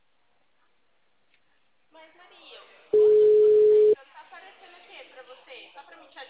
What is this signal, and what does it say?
Ringback tone (Europe, 425 Hz), with speech.